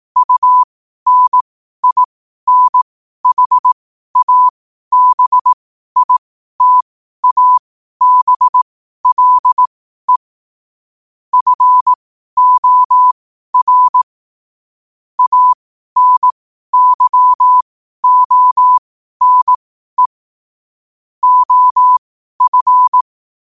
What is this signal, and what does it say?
Morse code, heard as a keyed tone, about 18 wpm, spaced as for 12 wpm, 1000 Hz. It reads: UNINHABITABLE FOR ANYONE OF
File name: qotdfiles / 12wpm